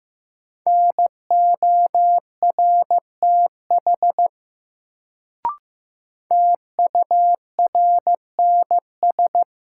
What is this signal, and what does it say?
Morse code, heard as a keyed tone, about 16 wpm, 700 Hz. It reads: NORTH TURNS
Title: Sets of 5 Words Encore - New Word Drill - 15wpm